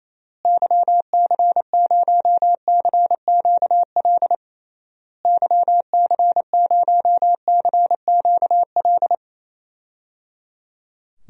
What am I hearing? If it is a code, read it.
Morse code, heard as a keyed tone, about 28 wpm, 700 Hz. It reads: YC0CQL YC0CQL
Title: DXCC-100 Call Signs - 6 Characters - Repeated 3x - 28wpm